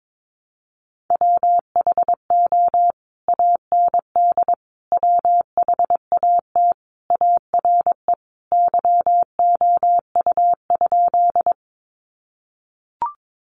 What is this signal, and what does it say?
Morse code, heard as a keyed tone, about 22 wpm, 700 Hz. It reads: WHO AND WHAT ARE YOU?
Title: Sentences from Top 100 Words 22wpm